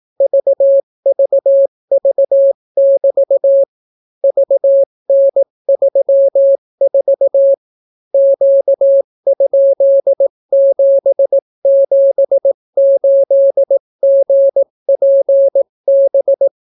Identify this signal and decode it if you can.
Morse code, heard as a keyed tone, about 18 wpm, 550 Hz. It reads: VVV<BT> VN34 Q?778GPB